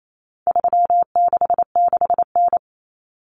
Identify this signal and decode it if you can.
Morse code, heard as a keyed tone, about 28 wpm, 700 Hz. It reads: 366N
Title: Common Cut Numbers - 4-Digit 28wpm